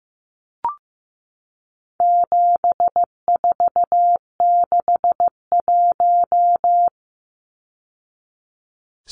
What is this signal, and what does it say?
Morse code, heard as a keyed tone, about 15 wpm, 700 Hz. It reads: 7461
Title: Numbers - 4-Digit 15wpm